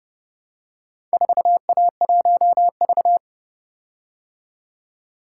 Morse code, heard as a keyed tone, about 29 wpm, 700 Hz. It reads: 4A1V